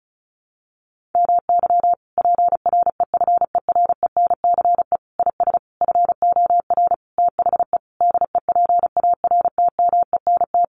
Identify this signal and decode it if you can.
Morse code, heard as a keyed tone, about 34 wpm, 700 Hz. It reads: MY PREFERENCE IS FOR THE DEPARTMENT
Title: Sentences from Top 1500 Words 35wpm